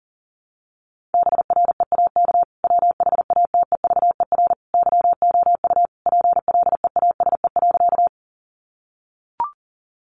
Morse code, heard as a keyed tone, about 40 wpm, 700 Hz. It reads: BREAK WHATEVER YOU PLEASE.
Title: Sentences from Top 1500 Words 40wpm